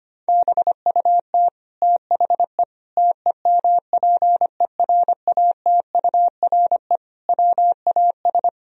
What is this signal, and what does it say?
Morse code, heard as a keyed tone, about 25 wpm, 700 Hz. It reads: BUT THE TEMPERATURE WAS
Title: Sentences from Top 1000 Words - Review Part 4 - 25wpm